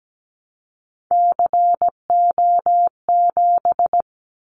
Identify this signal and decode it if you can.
Morse code, heard as a keyed tone, about 17 wpm, 700 Hz. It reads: CO7
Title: Callsign Prefixes for Top 100 DXCC - 17wpm